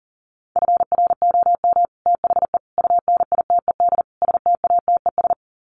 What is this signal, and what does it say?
Morse code, heard as a keyed tone, about 38 wpm, 700 Hz. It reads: FROM THE UNITED STATES
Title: Sets of 4 Words 40wpm